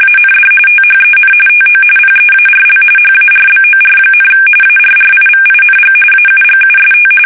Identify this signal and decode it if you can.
RTTY, about 45.45 baud. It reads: AND THEN TO PEARL HARBOR, ARRIVING AUGUST
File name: fsk850 / battleshipnj